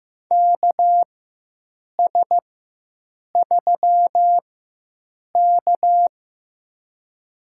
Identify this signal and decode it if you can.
Morse code, heard as a keyed tone, about 15 wpm, spaced as for 7 wpm, 700 Hz. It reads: KS3K
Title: USA Call Signs - 4 Characters - 4x Character Spacing - 15wpm